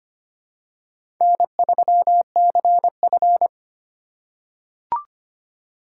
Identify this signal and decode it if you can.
Morse code, heard as a keyed tone, about 25 wpm, 700 Hz. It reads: N3CF
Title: USA Call Signs - 4 Characters - 3x Character Spacing - 25wpm